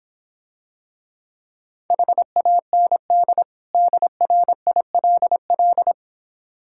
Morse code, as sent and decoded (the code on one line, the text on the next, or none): .... .- -. -.. / -.. .-. .. .-.. .-..
HAND DRILL